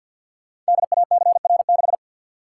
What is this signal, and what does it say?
Morse code, heard as a keyed tone, about 52 wpm, 700 Hz. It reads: DACR6
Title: Code Groups - 5 Letter:Numbers - 50wpm